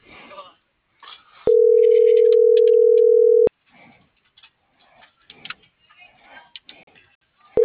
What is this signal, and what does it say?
Ringback tone (North America), 440 + 480 Hz.